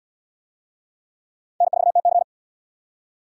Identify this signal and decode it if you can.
Morse code, heard as a keyed tone, about 78 wpm, 700 Hz. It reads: N4TC